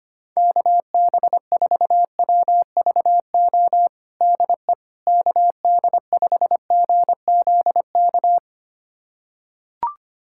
Morse code, heard as a keyed tone, about 25 wpm, 700 Hz. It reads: KB4WVO DE KD5GZK
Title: USA Callsign DE Callsign 25wpm